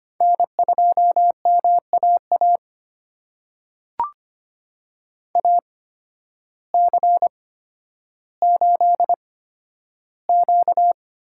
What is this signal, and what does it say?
Morse code, heard as a keyed tone, about 25 wpm, spaced as for 12 wpm, 700 Hz. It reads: N2MAA AC8Q
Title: USA Call Signs - 8x Character Spacing - 25wpm